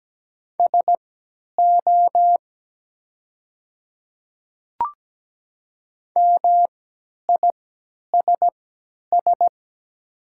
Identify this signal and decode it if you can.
Morse code, heard as a keyed tone, about 17 wpm, spaced as for 10 wpm, 700 Hz. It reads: SO MISS
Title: 3x Character Spacing - Top 500 Words - 17wpm